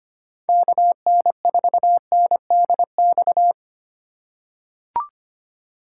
Morse code, heard as a keyed tone, about 25 wpm, 700 Hz. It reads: KN4NDX
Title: USA Call Signs - 6 Characters - 3x Character Spacing - 25wpm